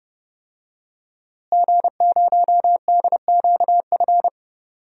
Morse code, heard as a keyed tone, about 30 wpm, 700 Hz. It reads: G0DQF
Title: DXCC-100 Call Signs - 5 Characters - Repeated 3x - 30wpm